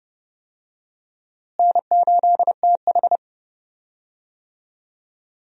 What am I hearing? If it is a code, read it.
Morse code, heard as a keyed tone, about 31 wpm, 700 Hz. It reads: N8TH
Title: USA Call Signs - 4 Characters - Repeated 3x - 30wpm